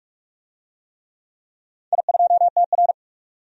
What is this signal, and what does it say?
Morse code, heard as a keyed tone, about 42 wpm, 700 Hz. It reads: I2TR
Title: DXCC-100 Call Signs - 4 Characters - 45wpm